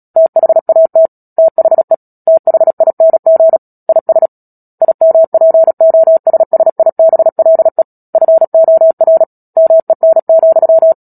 Morse code, heard as a keyed tone, about 37 wpm, 650 Hz. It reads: THAT THE THING IS IMPOSSIBLE FOR MEN,
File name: Book Two / 10-Chapter 10 - Epilogue - 36 WPM